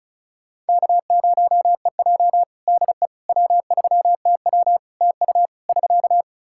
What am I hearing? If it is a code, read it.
Morse code, heard as a keyed tone, about 36 wpm, 700 Hz. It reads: K0EJ DE W3TW TU <SK>
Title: Practice QSO - Exchange 4 - Ending - 35wpm